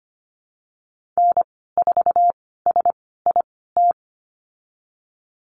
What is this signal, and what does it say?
Morse code, heard as a keyed tone, about 25 wpm, spaced as for 16 wpm, 700 Hz. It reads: N4SIT
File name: Lesson 15g - Number 5 - Call Signs 25wpm